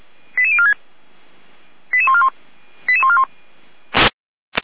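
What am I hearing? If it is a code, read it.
Five-tone selective call (ZVEI), heard as quick runs of five tones, tones of about 70 ms. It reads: 88046 88141 88141
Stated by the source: Five-tone ZVEI selective call 88149, received by SDR.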